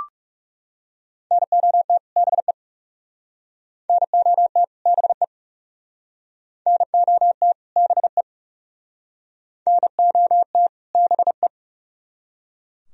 Morse code, heard as a keyed tone, about 39 wpm, 700 Hz. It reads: NOT BE NOT BE NOT BE NOT BE